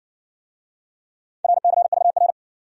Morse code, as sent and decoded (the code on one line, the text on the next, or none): .-. -.-. ...- .-.
RCVR